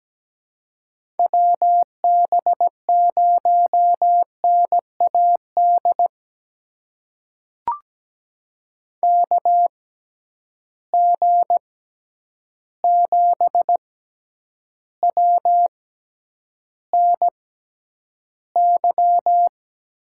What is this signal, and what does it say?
Morse code, heard as a keyed tone, about 17 wpm, spaced as for 9 wpm, 700 Hz. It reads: WB0NAD KG7WNY
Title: USA Call Signs - 6 Characters - 6x Character Spacing - 17wpm